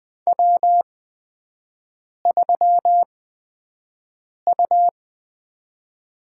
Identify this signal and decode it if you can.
Morse code, heard as a keyed tone, about 20 wpm, spaced as for 5 wpm, 700 Hz. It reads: W3U
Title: USA Call Signs - 5 Characters - 8x Character Spacing - 20wpm